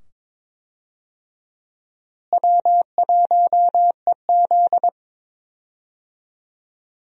Morse code, heard as a keyed tone, about 22 wpm, 700 Hz. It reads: W1EZ
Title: USA Call Signs - 4 Characters - Repeated 2x - 22wpm